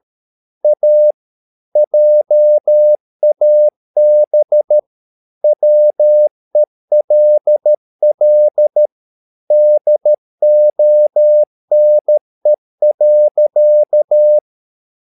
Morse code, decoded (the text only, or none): A JAB WELL DONE.